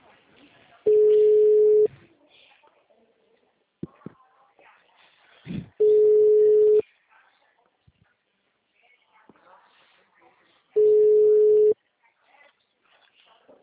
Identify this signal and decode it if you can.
Ringback tone (Europe, 425 Hz), with speech.
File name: out- -2002- - - .285